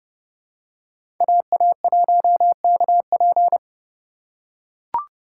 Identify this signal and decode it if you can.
Morse code, heard as a keyed tone, about 31 wpm, 700 Hz. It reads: AA1KP
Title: USA Call Signs - 5 Characters - 2x Character Spacing - 30wpm